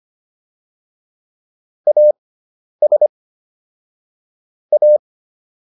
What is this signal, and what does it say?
Morse code, heard as a keyed tone, about 25 wpm, spaced as for 10 wpm, 600 Hz. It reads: AS A